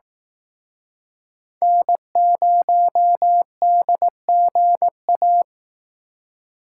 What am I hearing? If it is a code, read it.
Morse code, heard as a keyed tone, about 18 wpm, 700 Hz. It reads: N0DGA
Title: Speed Racing - USA Call Signs 15wpm